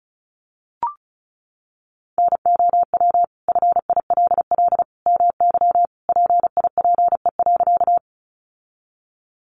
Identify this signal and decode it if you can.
Morse code, heard as a keyed tone, about 34 wpm, 700 Hz. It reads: NOW FILL MY PIPE.
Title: Sentences from Top 1500 Words 35wpm